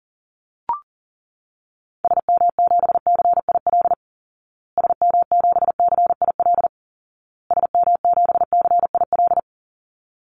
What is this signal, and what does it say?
Morse code, heard as a keyed tone, about 40 wpm, 700 Hz. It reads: SM7CIL SM7CIL SM7CIL
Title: Call Signs Repeated 2x - 40wpm